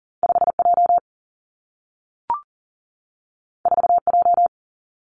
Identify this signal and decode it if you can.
Morse code, heard as a keyed tone, about 41 wpm, 700 Hz. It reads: 5J 4J
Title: Two Challenging Letter-Number Combinations 40wpm